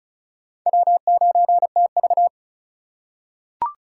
Morse code, heard as a keyed tone, about 34 wpm, 700 Hz. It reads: W9TV
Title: USA Call Signs - 4 Characters - 3x Character Spacing - 35wpm